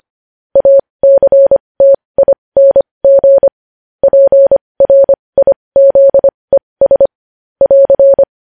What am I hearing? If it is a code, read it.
Morse code, heard as a keyed tone, about 25 wpm, spaced as for 20 wpm, 550 Hz. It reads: ACTING PRIZES <AR>
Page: news headlines in morse code at 20 wpm